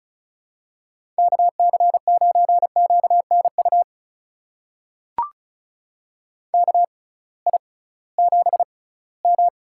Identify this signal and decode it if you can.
Morse code, heard as a keyed tone, about 36 wpm, spaced as for 21 wpm, 700 Hz. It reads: KC9QNU KI7M
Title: USA Call Signs - 6 Characters - 6x Character Spacing - 35wpm